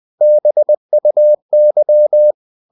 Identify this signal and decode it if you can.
Morse code, heard as a keyed tone, about 20 wpm, 600 Hz. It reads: BUY
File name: the old man and the sea part 2 20wpm